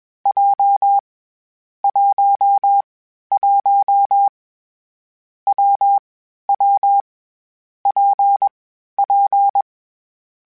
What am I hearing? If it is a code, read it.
Morse code, heard as a keyed tone, about 21 wpm, 800 Hz. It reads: J 1 1 W W P P
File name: Variable Speed / Variable Speed-mp3s / WPJ WPJ1 sp @20